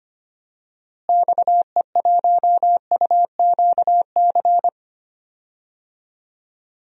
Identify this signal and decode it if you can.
Morse code, heard as a keyed tone, about 25 wpm, 700 Hz. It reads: XE1UQC
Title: DXCC-100 Call Signs - 6 Characters - Repeated 3x - 25wpm